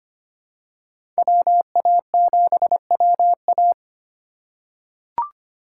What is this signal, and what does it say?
Morse code, heard as a keyed tone, about 25 wpm, 700 Hz. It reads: WA7WA
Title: USA Call Signs - 5 Characters - 3x Character Spacing - 25wpm